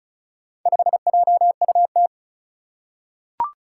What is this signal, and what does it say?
Morse code, heard as a keyed tone, about 36 wpm, 700 Hz. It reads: 5JUT
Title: Four Challenging Letter-Number Combinations 35wpm